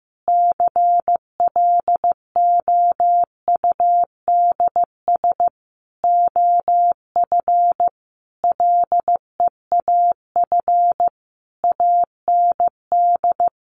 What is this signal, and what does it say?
Morse code, heard as a keyed tone, about 15 wpm, 700 Hz. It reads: CLOUDS OF LEAF AND
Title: A Sound Of Thunder 15wpm